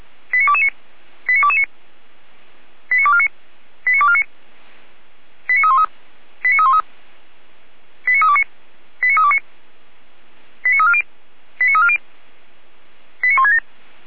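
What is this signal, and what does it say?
Five-tone selective call (ZVEI), heard as quick runs of five tones, tones of about 70 ms. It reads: 89229 89229 89249 89249 89313 89313 89329 89329 89340 89340 89167
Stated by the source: ZVEI five-tone call received by SDR.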